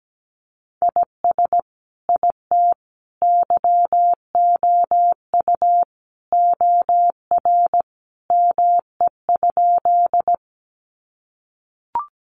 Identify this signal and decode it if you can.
Morse code, heard as a keyed tone, about 17 wpm, 700 Hz. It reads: IS IT YOU OR ME?
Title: Sentences from Top 200 Words 17wpm